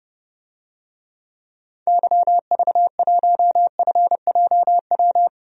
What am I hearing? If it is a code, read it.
Morse code, heard as a keyed tone, about 31 wpm, 700 Hz. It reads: YV1FJW
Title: DXCC-100 Call Signs - 6 Characters - 30wpm